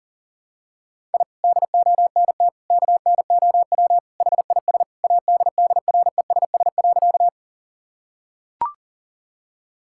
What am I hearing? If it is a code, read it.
Morse code, heard as a keyed tone, about 41 wpm, 700 Hz. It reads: I DONT KNOW HIS ADDRESS.
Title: Sentences from Top 1000 Words - Review Part 6 - 40wpm